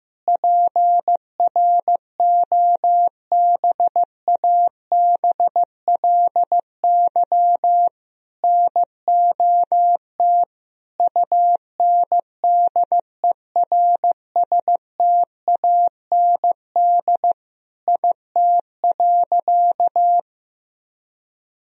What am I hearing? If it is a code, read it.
Morse code, heard as a keyed tone, about 16 wpm, 700 Hz. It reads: PROBABLY NOT UNDERSTAND IT.
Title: Sentences from Top 1000 Words - Review Part 4 - 15wpm